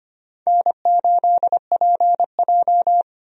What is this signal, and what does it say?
Morse code, heard as a keyed tone, about 25 wpm, 700 Hz. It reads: N8PJ
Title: USA Call Signs - 4 Characters - 25wpm